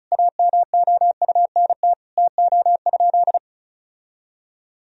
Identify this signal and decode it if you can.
Morse code, heard as a keyed tone, about 36 wpm, 700 Hz. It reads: AMOUNT TO?
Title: Sentences from Top 1000 Words - Review Part 4 - 35wpm